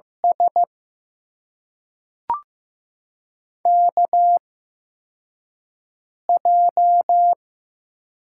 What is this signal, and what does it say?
Morse code, heard as a keyed tone, about 15 wpm, spaced as for 3 wpm, 700 Hz. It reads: SKJ
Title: USA Call Signs - 8x Character Spacing - 15wpm